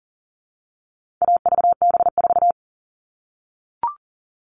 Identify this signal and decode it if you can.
Morse code, heard as a keyed tone, about 40 wpm, 700 Hz. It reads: AVB4
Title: All Cut Numbers - 4-Digit 40wpm